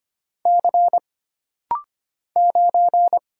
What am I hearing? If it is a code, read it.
Morse code, heard as a keyed tone, about 25 wpm, 700 Hz. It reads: C 9
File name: Single Character (Letter:Number:Punctuation:Prosign) - Rapid-Fire - 25wpm